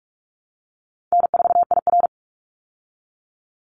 Morse code, heard as a keyed tone, about 44 wpm, 700 Hz. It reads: N4IR
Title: Easy Call Signs Repeated 2x - 45wpm